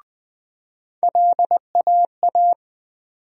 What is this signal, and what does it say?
Morse code, heard as a keyed tone, about 20 wpm, 700 Hz. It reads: LAA
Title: USA Call Sign Postfixes - 3 Letters 20wpm